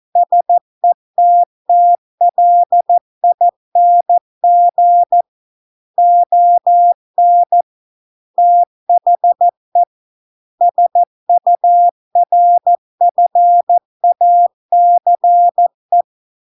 Morse code, heard as a keyed tone, about 14 wpm, 700 Hz. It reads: SETTLING ON THE SURFACE